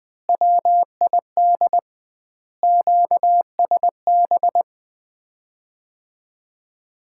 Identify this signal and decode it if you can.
Morse code, heard as a keyed tone, about 20 wpm, 700 Hz. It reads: WID QSB
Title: Common QSO Phrases - 2x Word-Spacing - 20wpm